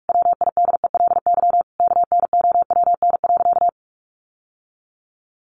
Morse code, heard as a keyed tone, about 44 wpm, 700 Hz. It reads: WIDELY KNOWN.
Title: Sentences from Top 1200 Words 45wpm